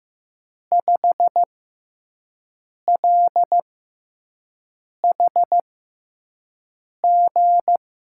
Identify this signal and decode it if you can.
Morse code, heard as a keyed tone, about 15 wpm, spaced as for 5 wpm, 700 Hz. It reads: 5LHG
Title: USA Call Signs - 5 Characters - 6x Character Spacing - 15wpm